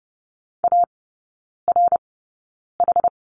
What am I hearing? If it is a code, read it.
Morse code, heard as a keyed tone, about 30 wpm, spaced as for 9 wpm, 700 Hz. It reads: ARH
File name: 081 Lesson 14e - Number 2 - Three Characters 30wpm